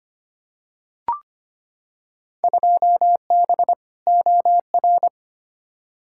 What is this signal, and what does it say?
Morse code, heard as a keyed tone, about 25 wpm, 700 Hz. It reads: 2B OR
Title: Field Day Exchanges Encore - 25wpm